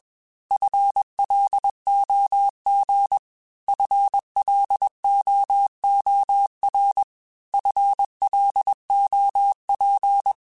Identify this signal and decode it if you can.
Morse code, heard as a keyed tone, about 21 wpm, 800 Hz. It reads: FLOG FLOOR FLOP
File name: FL Runs / FL Runs mp3s / FL Runs @20